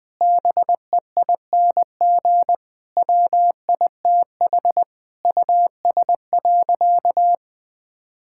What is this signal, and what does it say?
Morse code, heard as a keyed tone, about 21 wpm, 700 Hz. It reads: BEING WITH US.
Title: Sentences with 5-Word Phrases - 20wpm